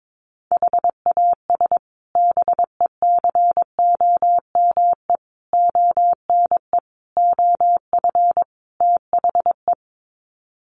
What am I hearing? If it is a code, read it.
Morse code, heard as a keyed tone, about 22 wpm, 700 Hz. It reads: HAS BECOME ONE OF THE
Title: Sets of 5 Words Encore 22wpm